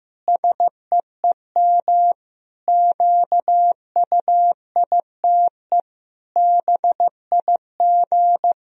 Morse code, heard as a keyed tone, about 15 wpm, 700 Hz. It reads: SEEM QUITE BIG
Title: Sentences from Top 1000 Words - Review Part 2 - 15wpm